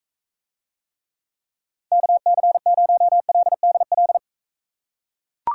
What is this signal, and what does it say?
Morse code, heard as a keyed tone, about 43 wpm, 700 Hz. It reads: KC0LDL